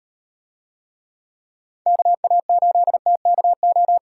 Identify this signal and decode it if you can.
Morse code, heard as a keyed tone, about 37 wpm, 700 Hz. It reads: KA8TKO